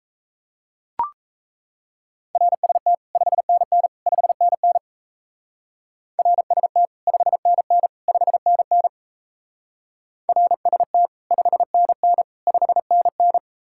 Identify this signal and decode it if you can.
Morse code, heard as a keyed tone, about 34 wpm, 700 Hz. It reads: RST 5NN 5NN RST 5NN 5NN RST 5NN 5NN
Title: Speed-Racing Common QSO Phrases Encore - 28wpm